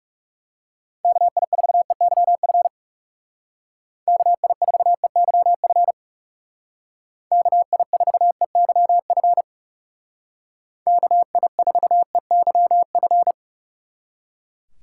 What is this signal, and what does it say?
Morse code, heard as a keyed tone, about 35 wpm, 700 Hz. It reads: KI4EYF KI4EYF KI4EYF KI4EYF
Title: Speed Racing - USA 2x3 Call Signs 30wpm